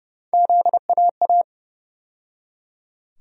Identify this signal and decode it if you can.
Morse code, heard as a keyed tone, about 30 wpm, 700 Hz. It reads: ZAA
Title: USA Call Sign Postfixes - 3 Letters 30wpm